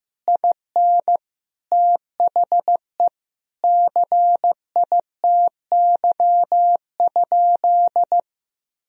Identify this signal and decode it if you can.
Morse code, heard as a keyed tone, about 15 wpm, 700 Hz. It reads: IN THE CITY?
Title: Sentences from Top 1000 Words - Review Part 1 - 15wpm